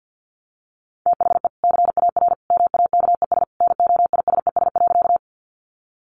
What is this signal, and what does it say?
Morse code, heard as a keyed tone, about 50 wpm, 700 Hz. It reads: THE CAR MAKES NOISES.
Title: Easy Phrases 50wpm